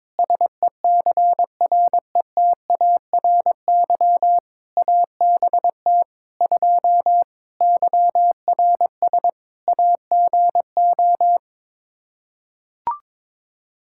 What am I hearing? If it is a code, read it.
Morse code, heard as a keyed tone, about 22 wpm, 700 Hz. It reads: SECRETARY ABT 2 YRS AGO
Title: Practice QSO - Exchange 3 - More Info - 22wpm